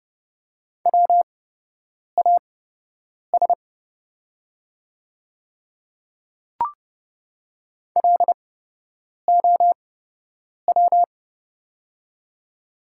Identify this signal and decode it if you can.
Morse code, heard as a keyed tone, about 30 wpm, spaced as for 8 wpm, 700 Hz. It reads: WAS LOW
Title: 8x Character Spacing - Top 300 Words - 30wpm